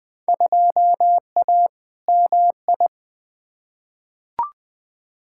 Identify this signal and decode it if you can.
Morse code, heard as a keyed tone, about 20 wpm, 700 Hz. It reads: 2A MI
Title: Field Day Exchanges Encore - 20wpm